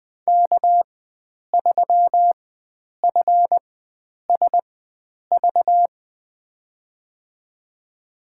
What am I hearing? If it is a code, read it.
Morse code, heard as a keyed tone, about 20 wpm, spaced as for 9 wpm, 700 Hz. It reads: K3FSV
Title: USA Call Signs - 5 Characters - 4x Character Spacing - 20wpm